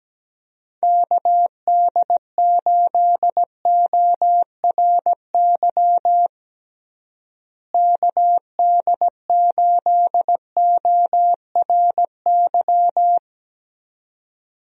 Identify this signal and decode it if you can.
Morse code, heard as a keyed tone, about 17 wpm, 700 Hz. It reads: KD8ORY KD8ORY
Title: USA Call Signs - 6 Characters - Repeated 2x - 17wpm